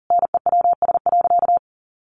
Morse code, heard as a keyed tone, about 40 wpm, 700 Hz. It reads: NEWS.
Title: Sentences from Top 1100 Words 40wpm